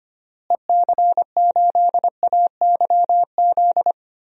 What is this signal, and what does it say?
Morse code, heard as a keyed tone, about 25 wpm, 700 Hz. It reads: EC8AYZ
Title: DXCC-100 Call Signs - 6 Characters - Repeated 3x - 25wpm